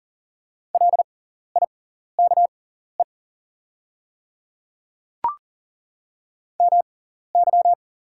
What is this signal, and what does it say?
Morse code, heard as a keyed tone, about 40 wpm, spaced as for 14 wpm, 700 Hz. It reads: LIKE MY
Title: 6x Character Spacing - Top 100 Words - 40wpm